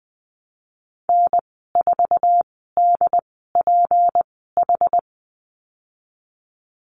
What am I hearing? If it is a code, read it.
Morse code, heard as a keyed tone, about 20 wpm, spaced as for 15 wpm, 700 Hz. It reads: N4DPH
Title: 167 Lesson 27g - Number 7 - Call Signs 20wpm